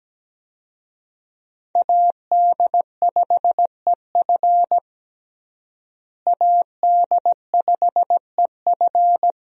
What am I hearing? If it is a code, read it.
Morse code, heard as a keyed tone, about 17 wpm, 700 Hz. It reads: AD5EF AD5EF
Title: USA Call Signs - 5 Characters - Repeated 2x - 17wpm